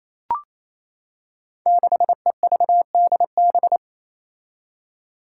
Morse code, heard as a keyed tone, about 28 wpm, 700 Hz. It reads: 6EVDB
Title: Five Challenging Letter-Number Combinations 28wpm